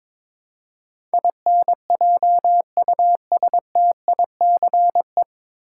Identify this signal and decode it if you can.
Morse code, heard as a keyed tone, about 22 wpm, 700 Hz. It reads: INJUSTICE
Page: Morse Code Ninja practice 22wpm